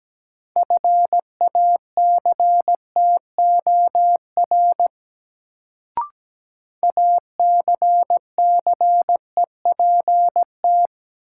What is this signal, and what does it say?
Morse code, heard as a keyed tone, about 17 wpm, 700 Hz. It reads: FACTOR ACCEPT
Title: Sentences from Top 1300 Words - New Word Drill - 17wpm